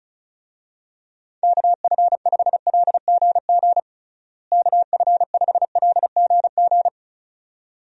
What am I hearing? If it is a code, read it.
Morse code, heard as a keyed tone, about 34 wpm, 700 Hz. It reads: KF5LGG KF5LGG
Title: USA Call Signs - 6 Characters - Repeated 2x - 35wpm